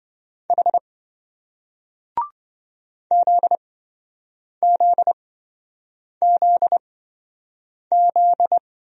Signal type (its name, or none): Morse code